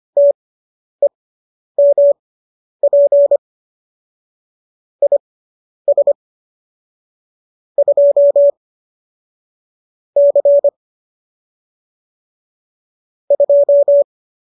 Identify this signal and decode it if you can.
Morse code, heard as a keyed tone, about 25 wpm, spaced as for 10 wpm, 570 Hz. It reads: TEMP IS 2 C 2
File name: qso219 10